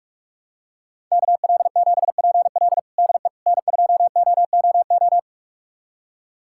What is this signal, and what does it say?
Morse code, heard as a keyed tone, about 41 wpm, 700 Hz. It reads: KL7PL DE N2OOO
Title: USA Callsign DE Callsign - 5 Characters - 45wpm